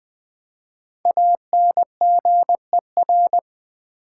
Morse code, heard as a keyed tone, about 20 wpm, 700 Hz. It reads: ANGER
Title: Five Letter Words - 20wpm